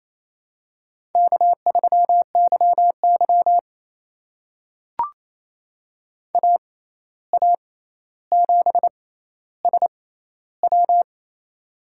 Morse code, heard as a keyed tone, about 27 wpm, spaced as for 13 wpm, 700 Hz. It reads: K3YY AA7SW
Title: USA Call Signs - 6x Character Spacing - 28wpm